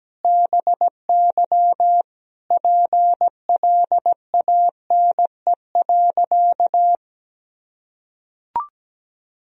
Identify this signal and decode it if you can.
Morse code, heard as a keyed tone, about 17 wpm, 700 Hz. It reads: BY PLANE.
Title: Sentences from Top 1000 Words - Review Part 3 - 17wpm